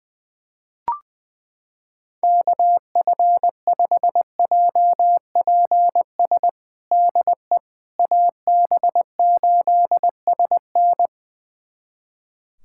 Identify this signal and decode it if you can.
Morse code, heard as a keyed tone, about 21 wpm, 700 Hz. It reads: KF5JPS DE AB8SN
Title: USA Callsign DE Callsign 20wpm